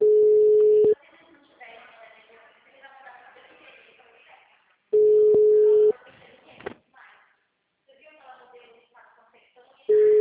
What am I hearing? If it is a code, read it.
Ringback tone (Europe, 425 Hz), with speech.